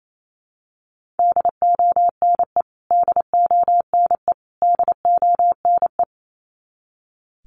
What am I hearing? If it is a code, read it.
Morse code, heard as a keyed tone, about 28 wpm, 700 Hz. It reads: DONE DONE DONE
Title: Top 500 Words Repeated 28wpm